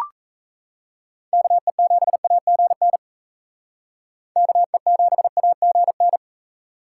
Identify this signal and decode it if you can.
Morse code, heard as a keyed tone, about 35 wpm, 700 Hz. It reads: KE7AGN KE7AGN
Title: Speed Racing - USA 2x3 Call Signs 28wpm